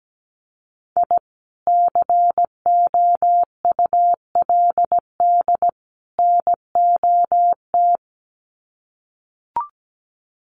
Morse code, heard as a keyed tone, about 17 wpm, 700 Hz. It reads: I COULD NOT
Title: Sets of 3 Words 17wpm